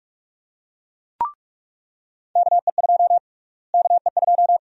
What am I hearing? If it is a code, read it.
Morse code, heard as a keyed tone, about 43 wpm, 700 Hz. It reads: KE2 KE2
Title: USA Call Sign Prefixes - Repeated 2x - 45wpm